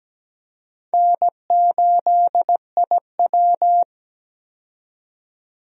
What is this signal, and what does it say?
Morse code, heard as a keyed tone, about 17 wpm, 700 Hz. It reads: N8IW
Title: USA Call Signs - 4 Characters - 17wpm